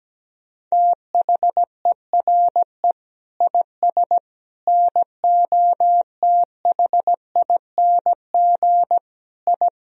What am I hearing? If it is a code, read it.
Morse code, heard as a keyed tone, about 17 wpm, 700 Hz. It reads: THERE IS NOTHING I
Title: Sentences from Top 1000 Words - Review Part 1 - 17wpm